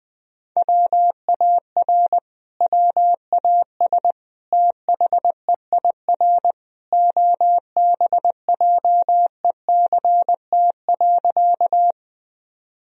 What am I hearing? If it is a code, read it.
Morse code, heard as a keyed tone, about 20 wpm, 700 Hz. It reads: WAR WAS THEIR OBJECT.
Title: Sentences from Top 1000 Words - Review Part 3 - 20wpm